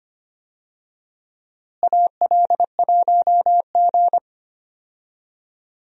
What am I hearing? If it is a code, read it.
Morse code, heard as a keyed tone, about 25 wpm, 700 Hz. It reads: AL1G
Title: USA Call Signs - 4 Characters - Repeated 2x - 25wpm